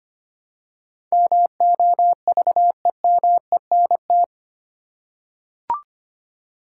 Morse code, heard as a keyed tone, about 25 wpm, 700 Hz. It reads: MOVEMENT
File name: Eight Letter Words - 25wpm